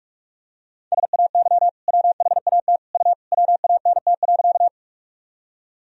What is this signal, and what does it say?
Morse code, heard as a keyed tone, about 42 wpm, 700 Hz. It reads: SAY WHAT U WANT.
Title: Sentences from Top 200 Words and Abbreviations 45wpm